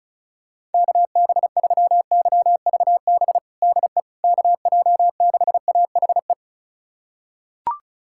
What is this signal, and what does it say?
Morse code, heard as a keyed tone, about 36 wpm, 700 Hz. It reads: KB3YVB DE KJ6AHE
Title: USA Callsign DE Callsign - 6 Characters - 35wpm